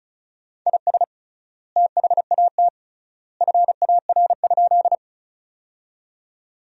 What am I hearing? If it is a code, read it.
Morse code, heard as a keyed tone, about 36 wpm, 700 Hz. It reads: IS THAT FAR?
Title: 3x Word-Spacing - Sentences from Top 300 Words - 35wpm